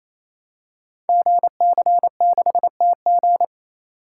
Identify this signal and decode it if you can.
Morse code, heard as a keyed tone, about 27 wpm, 700 Hz. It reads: GC6TG
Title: Code Groups - 5 Letter:Numbers - 28wpm